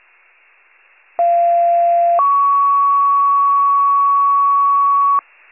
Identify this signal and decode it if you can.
Two-tone page, heard as a one-second tone, then a three-second tone, about 690 then 1120 Hz, with noise. It is heard from 1.2 to 5.2 s.
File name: pager / 41 RESCUE / wcalert